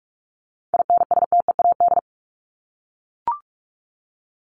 Morse code, heard as a keyed tone, about 44 wpm, 700 Hz. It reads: INSTEAD
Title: Top 501-750 Words 45wpm